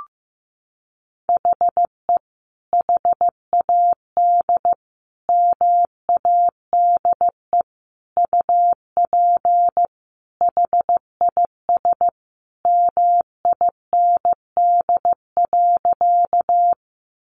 Morse code, heard as a keyed tone, about 15 wpm, 700 Hz. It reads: HE HAD MADE UP HIS MIND.
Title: Sentences from Top 500 Words 15wpm